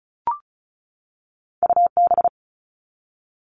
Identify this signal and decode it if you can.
Morse code, heard as a keyed tone, about 36 wpm, 700 Hz. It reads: UB
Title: All Cut Numbers - 2-Digit 35wpm